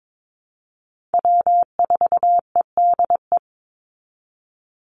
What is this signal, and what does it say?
Morse code, heard as a keyed tone, about 22 wpm, 700 Hz. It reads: W4EDE
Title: Call Signs Repeated 2x - 22wpm